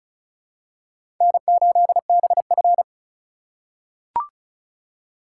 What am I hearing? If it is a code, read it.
Morse code, heard as a keyed tone, about 36 wpm, 700 Hz. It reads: N8BF
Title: USA Call Signs - 4 Characters - 4x Character Spacing - 35wpm